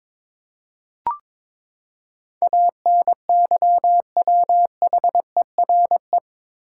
Morse code, heard as a keyed tone, about 21 wpm, 700 Hz. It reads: ANYWHERE
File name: practice / 22wpm / Eight Letter Words - 22wpm